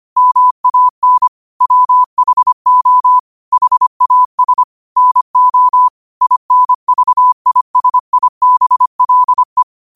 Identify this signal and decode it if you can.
Morse code, heard as a keyed tone, about 25 wpm, 1000 Hz. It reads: MAN WHO HAS NO INVISIBLE